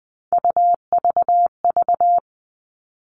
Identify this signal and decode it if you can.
Morse code, heard as a keyed tone, about 20 wpm, 700 Hz. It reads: UVV
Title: All Cut Numbers - 3-Digit 20wpm